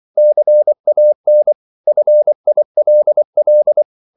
Morse code, heard as a keyed tone, about 24 wpm, 600 Hz. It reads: CAN FILL
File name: CW / Practice / If24